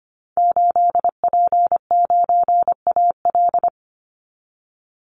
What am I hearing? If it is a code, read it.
Morse code, heard as a keyed tone, about 25 wpm, 700 Hz. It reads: 8P9AL
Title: Call Signs Repeated 3x - 25wpm